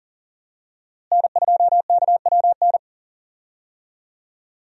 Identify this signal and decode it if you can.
Morse code, heard as a keyed tone, about 40 wpm, 700 Hz. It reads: N2KWN
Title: USA Call Signs - 5 Characters - Repeated 3x - 40wpm